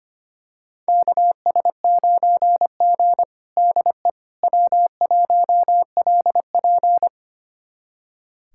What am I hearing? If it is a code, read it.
Morse code, heard as a keyed tone, about 25 wpm, 700 Hz. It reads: KS9G DE W1LP
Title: USA Callsign DE Callsign - 4 Characters - 25wpm